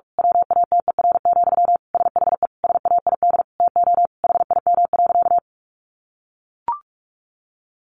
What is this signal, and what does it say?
Morse code, heard as a keyed tone, about 44 wpm, 700 Hz. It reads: WATER, SHE SAID TO HIM.
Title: Sentences from Top 800 Words 45wpm